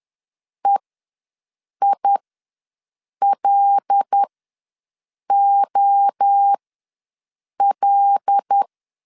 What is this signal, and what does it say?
Morse code, heard as a keyed tone, about 10 wpm, spaced as for 6 wpm, 800 Hz. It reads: EILOL
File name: LEZIONI / 6 PKX0 / 6 P ws 10wpm 3751